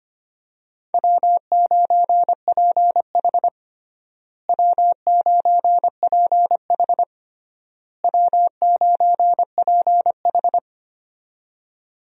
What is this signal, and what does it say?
Morse code, heard as a keyed tone, about 25 wpm, 700 Hz. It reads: W9PH W9PH W9PH
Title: USA Call Signs - 4 Characters - Repeated 3x - 25wpm